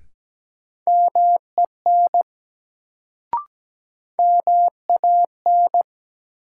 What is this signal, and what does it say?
Morse code, heard as a keyed tone, about 17 wpm, 700 Hz. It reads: MEN MAN
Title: Sentences from Top 200 Words - New Word Drill - 17wpm